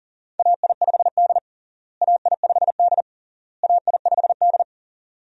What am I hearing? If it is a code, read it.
Morse code, heard as a keyed tone, about 40 wpm, 700 Hz. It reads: AI5D AI5D AI5D
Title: USA Call Signs - 4 Characters - Repeated 3x - 40wpm